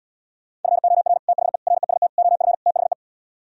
Morse code, heard as a keyed tone, about 77 wpm, 700 Hz. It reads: FOR THE REST OF THE